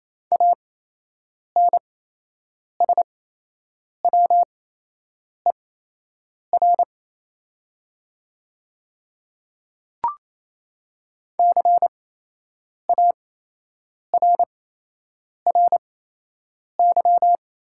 Morse code, heard as a keyed tone, about 28 wpm, spaced as for 8 wpm, 700 Hz. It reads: ANSWER CARRY
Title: 8x Character Spacing - Top 300 Words - 28wpm